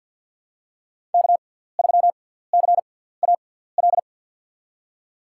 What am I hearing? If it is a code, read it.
Morse code, heard as a keyed tone, about 53 wpm, spaced as for 17 wpm, 700 Hz. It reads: K3CAL
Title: USA Call Signs - 5 Characters - 6x Character Spacing - 50wpm